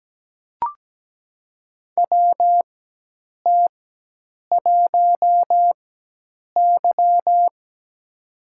Morse code, heard as a keyed tone, about 17 wpm, spaced as for 8 wpm, 700 Hz. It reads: WT1Y
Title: USA Call Signs - 4x Character Spacing - 17wpm